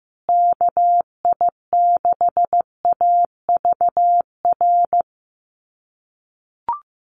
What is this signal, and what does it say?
Morse code, heard as a keyed tone, about 15 wpm, 700 Hz. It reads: KI6AVR
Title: USA Call Signs Repeated 2x 15wpm